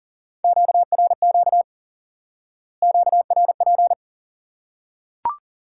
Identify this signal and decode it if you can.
Morse code, heard as a keyed tone, about 40 wpm, 700 Hz. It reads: QRQ QRP
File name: IQE Course 3 - Lesson 89 - YAGI - Two Elements - 40wpm